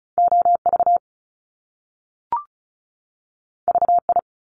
Morse code, heard as a keyed tone, about 36 wpm, 700 Hz. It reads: OV VI
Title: Top Two Letter Combinations in Words 35wpm